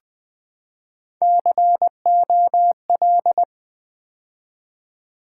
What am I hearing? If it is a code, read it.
Morse code, heard as a keyed tone, about 20 wpm, 700 Hz. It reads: COL
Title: Build A Word - 3 Letters At A Time - 9-Letter Words - 20wpm